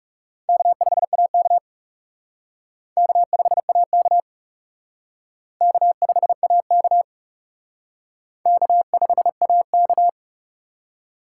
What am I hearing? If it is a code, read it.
Morse code, heard as a keyed tone, about 40 wpm, 700 Hz. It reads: K5AK K5AK K5AK K5AK